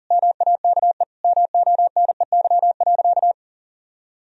Morse code, heard as a keyed tone, about 37 wpm, 700 Hz. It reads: MAKE MONEY.
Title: Sentences from Top 1000 Words - Review Part 4 - 40wpm